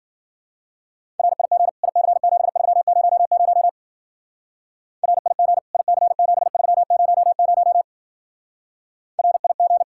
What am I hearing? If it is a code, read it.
Morse code, heard as a keyed tone, about 47 wpm, 700 Hz. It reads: RIG IC7300 RIG IC7300 RIG
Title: Speed-Racing Common QSO Phrases Encore - 40wpm